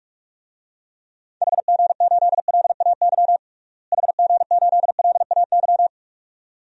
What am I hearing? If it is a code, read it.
Morse code, heard as a keyed tone, about 46 wpm, 700 Hz. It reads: HG8LAY HG8LAY
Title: DXCC-100 Call Signs - 6 Characters - Repeated 2x - 45wpm